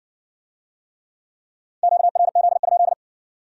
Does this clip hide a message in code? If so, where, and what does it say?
Yes: Morse code from 1.8 to 2.9 s, heard as a keyed tone, about 66 wpm, 700 Hz. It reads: QRZ?